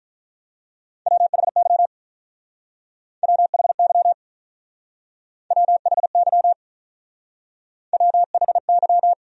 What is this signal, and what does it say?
Morse code, heard as a keyed tone, about 45 wpm, 700 Hz. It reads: WHY WHY WHY WHY